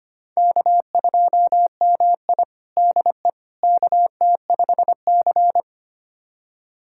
Morse code, heard as a keyed tone, about 25 wpm, 700 Hz. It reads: K2MI DE KT5C
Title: USA Callsign DE Callsign - 4 Characters - 25wpm